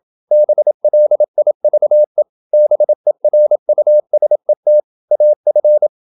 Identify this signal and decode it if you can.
Morse code, heard as a keyed tone, about 27 wpm, 600 Hz. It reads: BLIVE BERUSET AF
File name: Ferie QRQ 27wpm